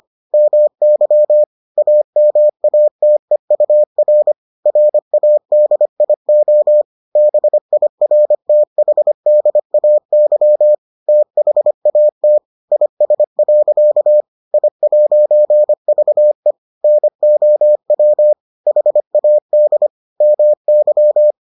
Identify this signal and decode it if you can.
Morse code, heard as a keyed tone, about 25 wpm, 600 Hz. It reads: MY AMATEUR RADIO BIRTHDAY THAT IS. I'VE NOW HAD MY